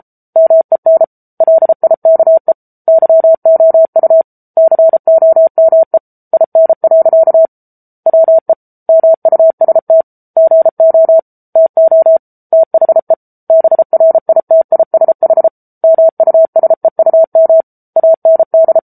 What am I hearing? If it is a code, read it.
Morse code, heard as a keyed tone, about 34 wpm, 650 Hz. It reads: MEN LIKE YOU COME IN. WE MUST GO TO THE BRITISH MUSEUM AND
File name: Book Two / 07-Chapter 7 - 33 WPM